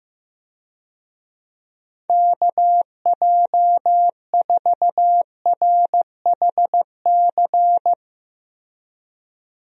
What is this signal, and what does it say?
Morse code, heard as a keyed tone, about 15 wpm, 700 Hz. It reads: KJ4RHC
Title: USA Call Signs - 6 Characters - Repeated 2x - 15wpm